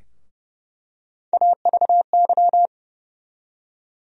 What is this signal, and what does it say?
Morse code, heard as a keyed tone, about 31 wpm, 700 Hz. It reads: AVY
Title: USA Call Sign Postfixes - 3 Letters 30wpm